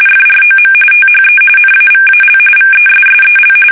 RTTY, about 45.45 baud. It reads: FROM THE NAVAL VESSEL
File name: fsk850 / USS-Massachusetts